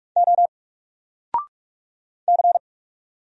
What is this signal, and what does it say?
Morse code, heard as a keyed tone, about 44 wpm, 700 Hz. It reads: O C
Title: Single Letter-Number 45wpm